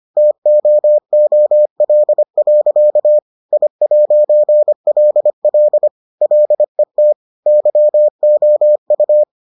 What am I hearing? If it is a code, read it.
Morse code, heard as a keyed tone, about 25 wpm, 600 Hz. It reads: TOOL. I'LL LET YOU